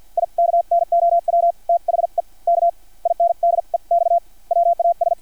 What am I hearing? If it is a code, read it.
Morse code, heard as a keyed tone, about 50 wpm, spaced as for 43 wpm, 680 Hz. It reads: I KNOW THE K INDEX WAS
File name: archived / CRSnet